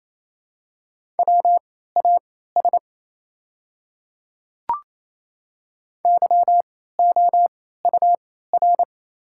Morse code, heard as a keyed tone, about 27 wpm, spaced as for 16 wpm, 700 Hz. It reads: WAS YOUR
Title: 3x Character Spacing - Top 300 Words - 28wpm